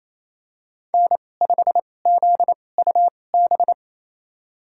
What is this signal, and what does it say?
Morse code, heard as a keyed tone, about 28 wpm, spaced as for 20 wpm, 700 Hz. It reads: N5ZUB
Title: USA Call Signs - 5 Characters - 2x Character Spacing - 28wpm